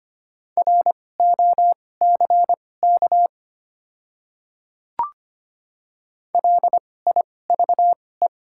Morse code, heard as a keyed tone, about 25 wpm, spaced as for 18 wpm, 700 Hz. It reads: ROCK LIVE
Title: 2x Character Spacing - Top 500 Words - 25wpm